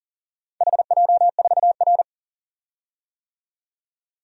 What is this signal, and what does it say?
Morse code, heard as a keyed tone, about 40 wpm, 700 Hz. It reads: HJ4R
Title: DXCC-100 Call Signs - 4 Characters - 40wpm